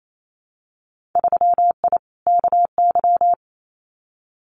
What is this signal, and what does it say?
Morse code, heard as a keyed tone, about 28 wpm, 700 Hz. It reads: 3I KY
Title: Winter Field Day Exchanges 28wpm